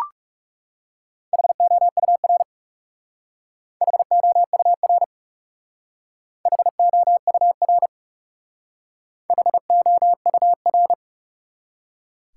Morse code, heard as a keyed tone, about 38 wpm, 700 Hz. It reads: HOUR HOUR HOUR HOUR